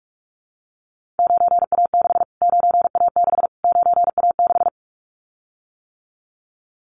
Morse code, heard as a keyed tone, about 44 wpm, 700 Hz. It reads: 9A6 9A6 9A6
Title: Callsign Prefixes for Top 25 DXCC - Repeated 3x - 45wpm